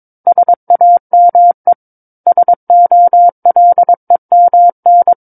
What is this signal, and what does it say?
Morse code, heard as a keyed tone, about 22 wpm, 700 Hz. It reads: SAME SOLEMN
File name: 06-Chapter-06 22 WPM